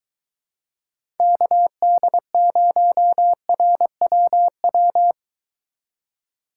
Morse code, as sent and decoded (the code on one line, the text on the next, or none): -.- -.. ----- .-. .-- .--
KD0RWW